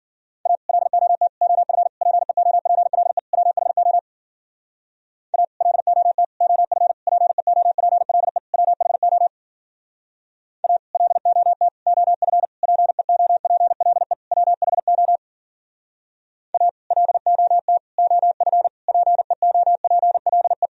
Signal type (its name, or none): Morse code